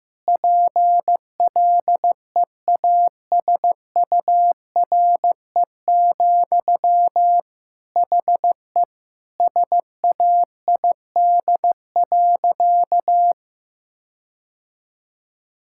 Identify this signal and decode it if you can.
Morse code, heard as a keyed tone, about 15 wpm, 700 Hz. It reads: PLEASURE, HE SAID.
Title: Sentences from Top 1600 Words 15wpm